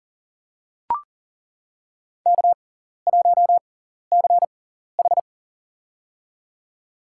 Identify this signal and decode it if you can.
Morse code, heard as a keyed tone, about 37 wpm, spaced as for 14 wpm, 700 Hz. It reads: K1CH
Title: USA Call Signs - 4 Characters - 6x Character Spacing - 40wpm